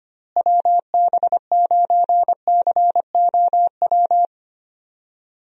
Morse code, heard as a keyed tone, about 25 wpm, 700 Hz. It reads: WB9COW
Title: USA Call Signs - 6 Characters - Repeated 2x - 25wpm